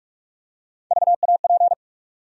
Morse code, heard as a keyed tone, about 43 wpm, 700 Hz. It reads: VAP
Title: USA Call Sign Postfixes - 3 Letters 45wpm